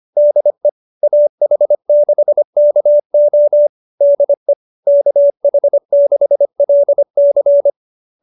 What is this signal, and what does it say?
Morse code, heard as a keyed tone, about 25 wpm, 590 Hz. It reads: DE AH6KO DE KH6LC